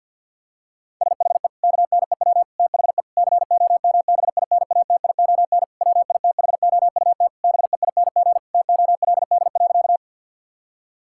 Morse code, heard as a keyed tone, about 52 wpm, 700 Hz. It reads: SHE KNEW THE COMBINATION WITHOUT BEING TOLD.
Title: Sentences from Top 1000 Words - Review Part 6 - 50wpm